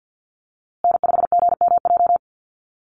Morse code, heard as a keyed tone, about 50 wpm, 700 Hz. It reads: N5GMJ
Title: USA Call Signs Repeated 2x 50wpm